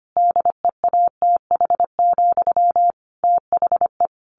Morse code, heard as a keyed tone, about 25 wpm, 700 Hz. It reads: DEATH, THE
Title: A Sound Of Thunder 25wpm